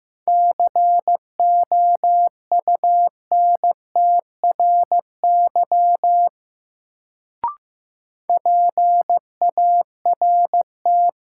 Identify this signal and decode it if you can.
Morse code, heard as a keyed tone, about 15 wpm, 700 Hz. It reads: COUNTRY PART
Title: Sets of 2 Words - New Word Drill - 15wpm